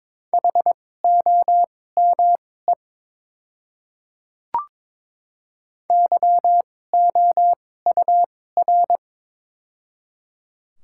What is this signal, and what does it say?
Morse code, heard as a keyed tone, about 22 wpm, spaced as for 16 wpm, 700 Hz. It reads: HOME YOUR
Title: 2x Character Spacing - Top 200 Words - 22wpm